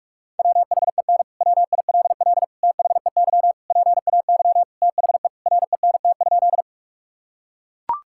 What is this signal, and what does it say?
Morse code, heard as a keyed tone, about 41 wpm, 700 Hz. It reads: WHEN WILL THEY PAY THE RENT?
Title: Sentences from Top 1000 Words - Review Part 5 - 45wpm